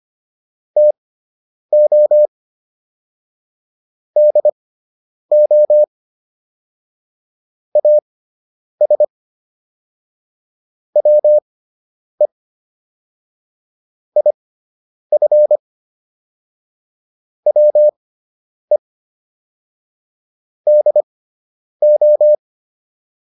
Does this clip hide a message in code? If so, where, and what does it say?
Yes: Morse code from 0.8 to 22.4 s, heard as a keyed tone, about 25 wpm, spaced as for 9 wpm, 600 Hz. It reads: TO DO AS WE IF WE DO